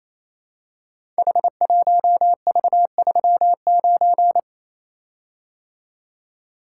Morse code, heard as a keyed tone, about 28 wpm, 700 Hz. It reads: H1V39
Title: Code Groups - 5 Letter:Numbers - 28wpm